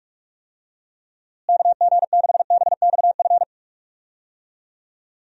Morse code, heard as a keyed tone, about 41 wpm, 700 Hz. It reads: KG6BXF
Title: USA Call Signs - 6 Characters - Repeated 3x - 45wpm